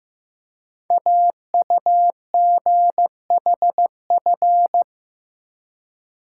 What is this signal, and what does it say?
Morse code, heard as a keyed tone, about 15 wpm, 700 Hz. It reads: AUGHF
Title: Code Groups - 5 Letters - 15wpm